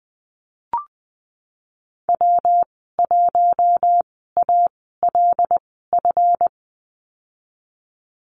Morse code, heard as a keyed tone, about 20 wpm, spaced as for 15 wpm, 700 Hz. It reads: W1ALF
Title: 186 Lesson 30g - Letter b - Call Signs 20wpm